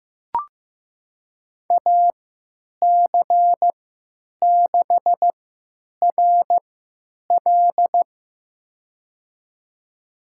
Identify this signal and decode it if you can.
Morse code, heard as a keyed tone, about 15 wpm, spaced as for 9 wpm, 700 Hz. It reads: AC6RL
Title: USA Call Signs - 5 Characters - 3x Character Spacing - 15wpm